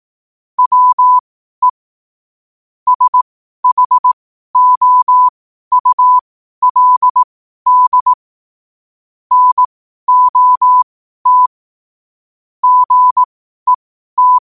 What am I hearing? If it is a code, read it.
Morse code, heard as a keyed tone, about 18 wpm, spaced as for 12 wpm, 1000 Hz. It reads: WE SHOULD NOT GET
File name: qotdfiles / 12wpm